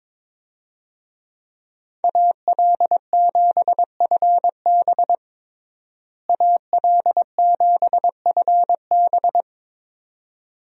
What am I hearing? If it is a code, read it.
Morse code, heard as a keyed tone, about 22 wpm, 700 Hz. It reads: AL7FB AL7FB
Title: USA Call Signs - 5 Characters - Repeated 2x - 22wpm